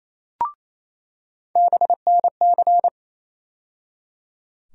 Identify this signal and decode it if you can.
Morse code, heard as a keyed tone, about 28 wpm, 700 Hz. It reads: BNC
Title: USA Call Sign Postfixes - 3 Letters 28wpm